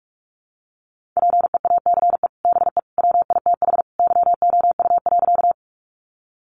Morse code, heard as a keyed tone, about 44 wpm, 700 Hz. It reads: PEACE BE WITH YOU.
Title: Sentences from Top 1500 Words 45wpm